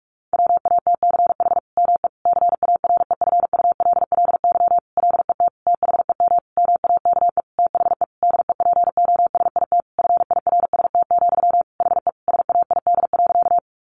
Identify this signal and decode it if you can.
Morse code, heard as a keyed tone, about 44 wpm, 700 Hz. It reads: WATCH ME CAREFULLY LET THEM MAKE THE DEPOSIT FIRST, HE SAID.
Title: Sentences from Top 1300 Words 45wpm